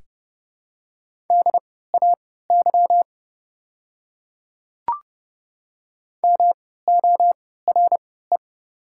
Morse code, heard as a keyed tone, about 28 wpm, spaced as for 17 wpm, 690 Hz. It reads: DAY MORE
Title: 3x Character Spacing - Top 100 Words - 30wpm